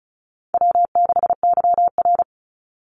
Morse code, heard as a keyed tone, about 34 wpm, 700 Hz. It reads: W6YR
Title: Easy Call Signs Repeated 3x - 35wpm